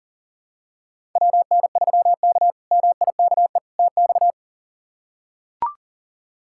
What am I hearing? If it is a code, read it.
Morse code, heard as a keyed tone, about 40 wpm, 700 Hz. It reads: WN3K MIKE TX
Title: K1USN SST QSO Practice - 40wpm